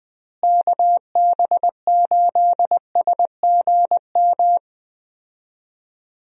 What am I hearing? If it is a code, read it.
Morse code, heard as a keyed tone, about 20 wpm, 700 Hz. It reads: KB8SGM
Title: USA Call Signs - 6 Characters - Repeated 3x - 20wpm